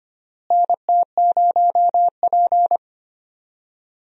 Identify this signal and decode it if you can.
Morse code, heard as a keyed tone, about 25 wpm, 700 Hz. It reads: NT0P
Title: USA Call Signs - 4 Characters - Repeated 3x - 25wpm